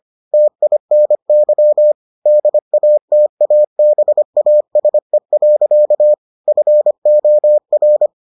Morse code, heard as a keyed tone, about 25 wpm, 600 Hz. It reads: TINY DATABASE. FOR